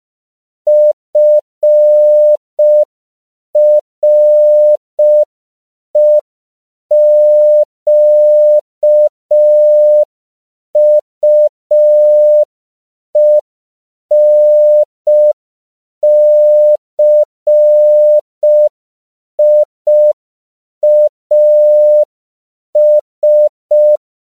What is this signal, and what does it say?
Morse code, heard as a keyed tone, about 5 wpm, 600 Hz. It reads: FREQUENCIAS
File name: CW / 5PPM